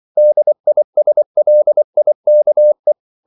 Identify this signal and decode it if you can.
Morse code, heard as a keyed tone, about 24 wpm, 600 Hz. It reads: DISLIKE
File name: CW / Practice / EnglishQuotations0424